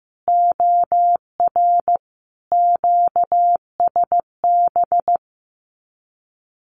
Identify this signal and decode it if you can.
Morse code, heard as a keyed tone, about 15 wpm, 700 Hz. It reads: OR QSB
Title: Common QSO Phrases 15wpm